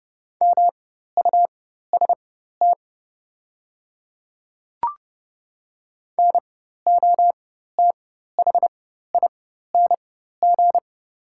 Morse code, heard as a keyed tone, about 30 wpm, spaced as for 14 wpm, 690 Hz. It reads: MUST NOTHING
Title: 4x Character Spacing - Top 500 Words - 30wpm